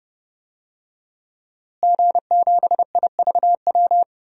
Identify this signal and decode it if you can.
Morse code, heard as a keyed tone, about 31 wpm, 700 Hz. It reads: G7IVW
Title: DXCC-100 Call Signs - 5 Characters - Repeated 3x - 30wpm